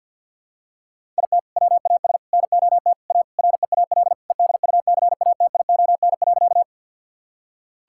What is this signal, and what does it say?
Morse code, heard as a keyed tone, about 52 wpm, 700 Hz. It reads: IT WAS NOT A REAL EDUCATION.
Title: Sentences from Top 1000 Words - Review Part 6 - 50wpm